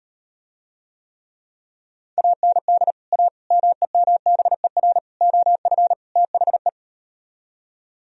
Morse code, heard as a keyed tone, about 38 wpm, 700 Hz. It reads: AND A MEMBER OF THE